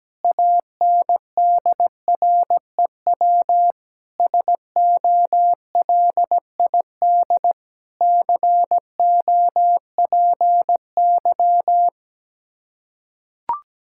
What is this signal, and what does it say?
Morse code, heard as a keyed tone, about 17 wpm, 700 Hz. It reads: ANDREW SOLID COPY
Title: Practice QSO - Exchange 3 - More Info - 17wpm